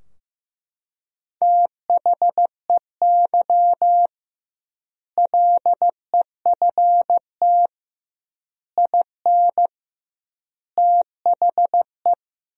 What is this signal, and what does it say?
Morse code, heard as a keyed tone, about 15 wpm, 700 Hz. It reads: THEY LEFT IN THE
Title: 2x Word-Spacing - Sentences from Top 400 Words - 15wpm